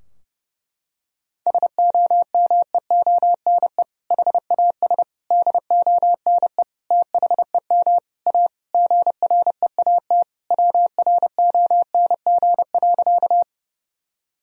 Morse code, heard as a keyed tone, about 32 wpm, 700 Hz. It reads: SOMEONE HAS DONE THEM A GREAT WRONG.
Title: Sentences from Top 1000 Words - Review Part 4 - 30wpm